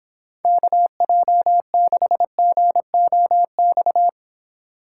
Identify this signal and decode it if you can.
Morse code, heard as a keyed tone, about 25 wpm, 700 Hz. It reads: KJ6GOX